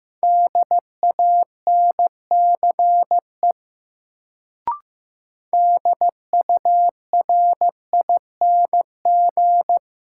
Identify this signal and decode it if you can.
Morse code, heard as a keyed tone, about 15 wpm, 700 Hz. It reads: DANCE DURING
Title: Sentences from Top 700 Words - New Word Drill - 15wpm